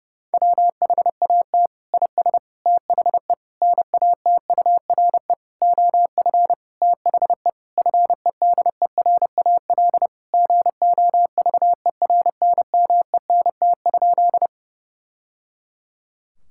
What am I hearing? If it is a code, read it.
Morse code, heard as a keyed tone, about 30 wpm, 700 Hz. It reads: WHAT IS THE NATURE OF THE FEDERAL GOVERNMENT?
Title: Sentences from Top 1000 Words - Review Part 5 - 30wpm